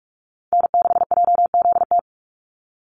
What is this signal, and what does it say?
Morse code, heard as a keyed tone, about 44 wpm, 700 Hz. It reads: N6JZT
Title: NA SOTA Chaser Callsigns 45wpm